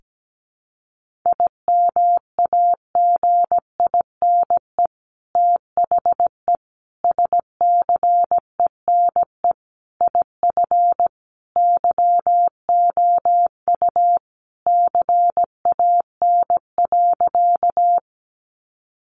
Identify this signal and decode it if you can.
Morse code, heard as a keyed tone, about 17 wpm, 700 Hz. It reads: IMAGINE THE SCENE IF YOU CAN.
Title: Sentences from Top 1400 Words 17wpm